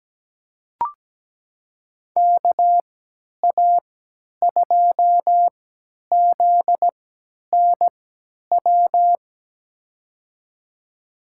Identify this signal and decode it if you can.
Morse code, heard as a keyed tone, about 17 wpm, spaced as for 10 wpm, 700 Hz. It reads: KA2ZNW
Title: USA Call Signs - 6 Characters - 3x Character Spacing - 17wpm